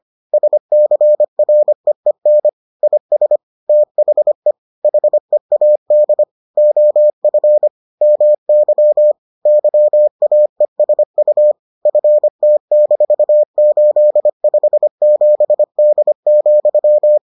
Morse code, heard as a keyed tone, about 25 wpm, 600 Hz. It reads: SCREEN IS THE HEAD OF MY YAESU FT-857D,